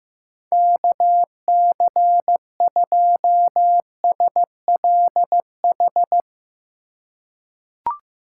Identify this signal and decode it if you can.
Morse code, heard as a keyed tone, about 16 wpm, 700 Hz. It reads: KC2SLH
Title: USA Call Signs - 6 Characters - 2x Character Spacing - 15wpm